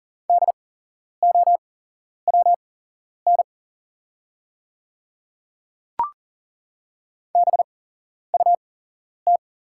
Morse code, heard as a keyed tone, about 40 wpm, spaced as for 11 wpm, 700 Hz. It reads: DOWN BUT
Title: 8x Character Spacing - Top 100 Words - 40wpm